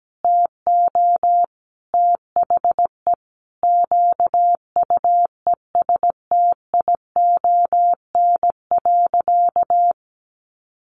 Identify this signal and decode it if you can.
Morse code, heard as a keyed tone, about 17 wpm, 700 Hz. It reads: TO THE QUESTION.
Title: Sentences from Top 1400 Words 17wpm